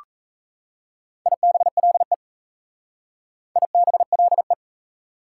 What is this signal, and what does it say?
Morse code, heard as a keyed tone, about 33 wpm, 700 Hz. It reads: IBLE IBLE
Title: Speed-Racing Prefixes and Suffixes 28wpm